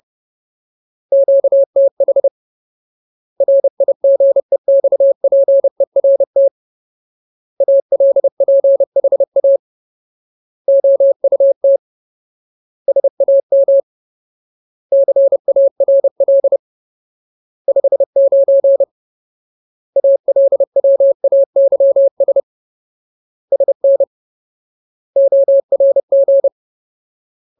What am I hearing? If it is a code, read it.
Morse code, heard as a keyed tone, about 30 wpm, 550 Hz. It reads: QTH RIGEXPERT ALPHA OUT SAM CARL 59 ALWAYS SN ORG
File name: Ultimate Word List / 30WPM Ultimate